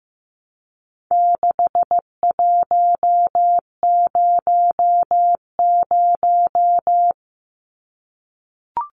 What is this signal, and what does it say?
Morse code, heard as a keyed tone, about 15 wpm, 700 Hz. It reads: 6100
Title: Numbers - 4-Digit 15wpm